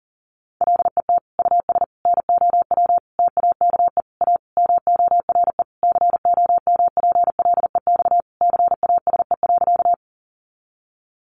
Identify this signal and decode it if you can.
Morse code, heard as a keyed tone, about 40 wpm, 700 Hz. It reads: LET US NOW TAKE A MORE COMPLEX CASE.
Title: Sentences from Top 900 Words 40wpm